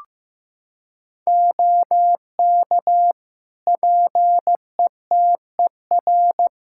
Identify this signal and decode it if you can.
Morse code, heard as a keyed tone, about 15 wpm, 700 Hz. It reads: OK PETER
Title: K1USN SST QSO Practice 15wpm